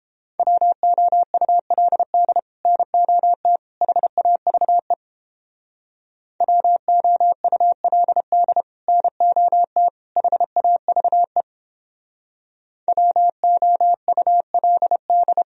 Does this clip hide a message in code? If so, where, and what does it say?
Yes: Morse code from 0.4 to 15.4 s, heard as a keyed tone, about 29 wpm, 700 Hz. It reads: WOULD NOT HAVE WOULD NOT HAVE WOULD